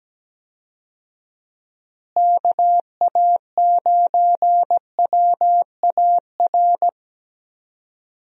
Morse code, heard as a keyed tone, about 17 wpm, 700 Hz. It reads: KA9WAR
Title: USA Call Signs - 6 Characters - Repeated 2x - 17wpm